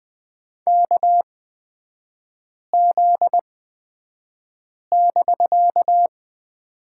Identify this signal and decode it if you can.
Morse code, heard as a keyed tone, about 20 wpm, spaced as for 5 wpm, 700 Hz. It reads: KZ<BK>
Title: 250 Lesson 40e - BK - Three Characters 20wpm